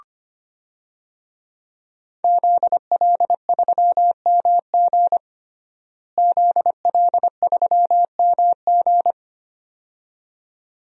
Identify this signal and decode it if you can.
Morse code, heard as a keyed tone, about 25 wpm, 700 Hz. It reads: ZL3MG ZL3MG
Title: DXCC-100 Call Signs - 5 Characters - Repeated 2x - 25wpm